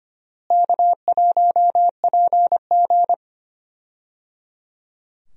Morse code, heard as a keyed tone, about 25 wpm, 700 Hz. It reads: K1PG
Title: USA Call Signs - 4 Characters - Repeated 3x - 25wpm